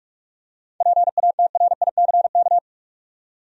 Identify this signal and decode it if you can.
Morse code, heard as a keyed tone, about 42 wpm, 700 Hz. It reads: PATRICK
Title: USA Common Names - 45wpm